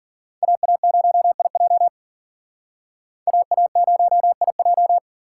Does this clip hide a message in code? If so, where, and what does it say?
Yes: Morse code from 0.4 to 5.0 s, heard as a keyed tone, about 51 wpm, 700 Hz. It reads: AA0IJ AA0IJ